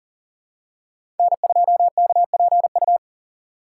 Morse code, heard as a keyed tone, about 40 wpm, 700 Hz. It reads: N2KPU
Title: USA Call Signs - 5 Characters - 2x Character Spacing - 40wpm